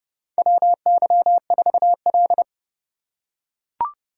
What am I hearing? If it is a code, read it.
Morse code, heard as a keyed tone, about 28 wpm, 700 Hz. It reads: WY4L
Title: USA Call Signs - 4 Characters - 3x Character Spacing - 30wpm